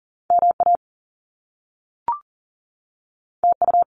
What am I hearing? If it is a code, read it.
Morse code, heard as a keyed tone, about 40 wpm, 700 Hz. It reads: MA TU
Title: Top Two Letter Combinations in Words 40wpm